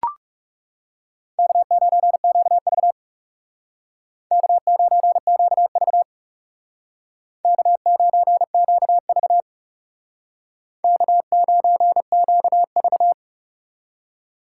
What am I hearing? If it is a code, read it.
Morse code, heard as a keyed tone, about 38 wpm, 700 Hz. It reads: K9QV K9QV K9QV K9QV